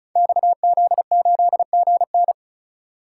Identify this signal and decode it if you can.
Morse code, heard as a keyed tone, about 36 wpm, 700 Hz. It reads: XZ8GN
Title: Code Groups - 5 Letter:Numbers - 35wpm